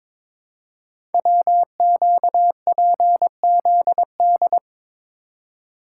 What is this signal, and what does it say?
Morse code, heard as a keyed tone, about 22 wpm, 700 Hz. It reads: WQPZD
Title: Code Groups - 5 Letter:Numbers - 22wpm